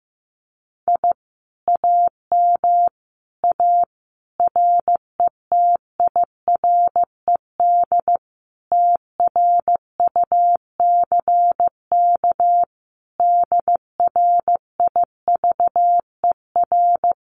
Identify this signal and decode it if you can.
Morse code, heard as a keyed tone, about 14 wpm, 700 Hz. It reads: I AM A RETIRED TRUCK DRIVER
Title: Common QSO Phrases Encore 15wpm